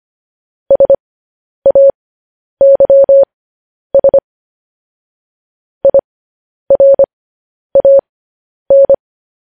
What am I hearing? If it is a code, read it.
Morse code, heard as a keyed tone, about 25 wpm, spaced as for 10 wpm, 550 Hz. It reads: SAYS IRAN